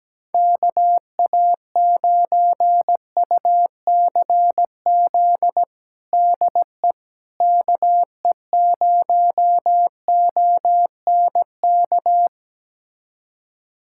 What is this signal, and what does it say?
Morse code, heard as a keyed tone, about 17 wpm, 700 Hz. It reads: KA9UCZ DE KE0ONK
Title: USA Callsign DE Callsign - 6 Characters - 17wpm